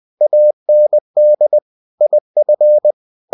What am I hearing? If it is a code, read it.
Morse code, heard as a keyed tone, about 20 wpm, 600 Hz. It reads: AND IF
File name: cw-books / winnie-the-pooh / winnie-20wpm-page-0085